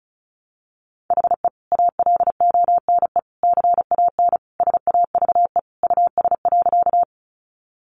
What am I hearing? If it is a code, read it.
Morse code, heard as a keyed tone, about 34 wpm, 700 Hz. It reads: HE ALONE CAN SAVE US.
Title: Sentences from Top 1100 Words 35wpm